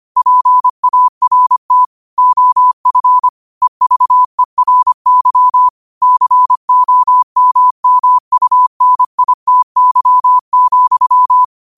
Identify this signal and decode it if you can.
Morse code, heard as a keyed tone, about 25 wpm, 1000 Hz. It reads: PART OF EVERY COMMUNITY,